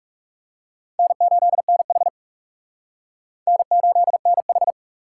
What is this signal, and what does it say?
Morse code, heard as a keyed tone, about 40 wpm, 700 Hz. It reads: N8NH N8NH